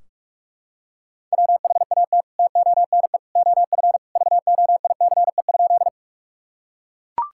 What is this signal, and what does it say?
Morse code, heard as a keyed tone, about 41 wpm, 700 Hz. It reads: WHAT TONE OF VOICE?
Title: Sentences from Top 1600 Words 45wpm